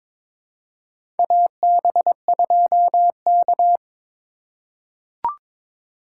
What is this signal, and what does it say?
Morse code, heard as a keyed tone, about 22 wpm, 700 Hz. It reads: AB2K
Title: USA Call Signs - 4 Characters - 8x Character Spacing - 22wpm